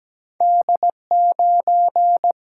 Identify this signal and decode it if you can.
Morse code, heard as a keyed tone, about 17 wpm, 700 Hz. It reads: D9
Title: Single-Letter Single-Number 17wpm